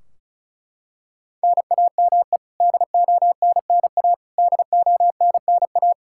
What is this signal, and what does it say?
Morse code, heard as a keyed tone, about 36 wpm, 700 Hz. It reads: NAME DONNA DONNA
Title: Practice QSO - Exchange 1 - Beginning - 35wpm